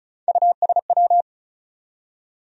Morse code, heard as a keyed tone, about 36 wpm, 700 Hz. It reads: USW
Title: USA Call Sign Postfixes - 3 Letters 35wpm